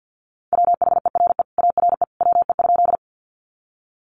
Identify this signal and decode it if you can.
Morse code, heard as a keyed tone, about 48 wpm, 700 Hz. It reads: WHERE ARE WE?
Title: Sentences from Top 200 Words 50wpm